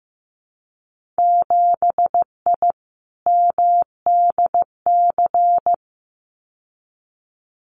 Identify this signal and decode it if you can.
Morse code, heard as a keyed tone, about 15 wpm, 700 Hz. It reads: 7I MDC
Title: Winter Field Day Exchanges Encore 15wpm